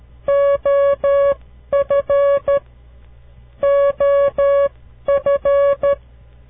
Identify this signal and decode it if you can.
Morse code, heard as a keyed tone, about 13 wpm, spaced as for 11 wpm, 560 Hz. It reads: OF OF
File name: CW Learning Programs / farnsworth / unit 2